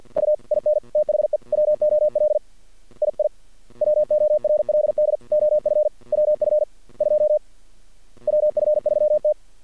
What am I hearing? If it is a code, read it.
Morse code, heard as a keyed tone, about 48 wpm, 600 Hz. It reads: U IN THE WOL II WORLD OF WF 2 WF3T
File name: archived / CRSnet